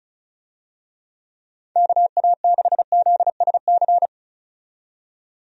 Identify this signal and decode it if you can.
Morse code, heard as a keyed tone, about 36 wpm, 700 Hz. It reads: KA6ZSC
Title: USA Call Signs - 6 Characters - 35wpm